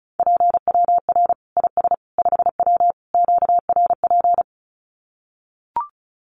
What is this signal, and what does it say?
Morse code, heard as a keyed tone, about 34 wpm, 700 Hz. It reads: PWR IS 5W QRP
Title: Common QSO Phrases 35wpm